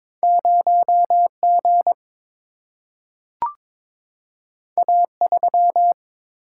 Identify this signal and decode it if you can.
Morse code, heard as a keyed tone, about 23 wpm, 700 Hz. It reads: 0 G A 3
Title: Single-Letter Single-Number 22wpm